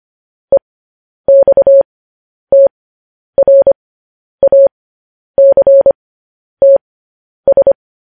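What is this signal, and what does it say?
Morse code, heard as a keyed tone, about 25 wpm, spaced as for 10 wpm, 550 Hz. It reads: EXTRACTS
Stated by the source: Morse transmission of news headlines.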